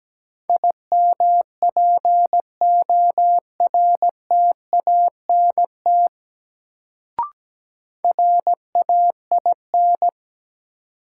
Morse code, heard as a keyed tone, about 17 wpm, 700 Hz. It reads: IMPORTANT RAIN
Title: Sentences from Top 600 Words - New Word Drill - 17wpm